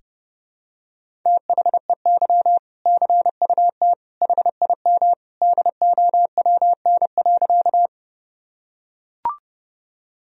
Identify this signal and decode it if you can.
Morse code, heard as a keyed tone, about 30 wpm, 700 Hz. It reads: THEY CUT HIM DOWN.
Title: Sentences from Top 1000 Words - Review Part 1 - 30wpm